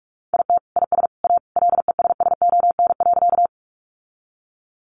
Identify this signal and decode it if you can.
Morse code, heard as a keyed tone, about 44 wpm, 700 Hz. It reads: IT IS A LESSON.
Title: Sentences from Top 1200 Words 45wpm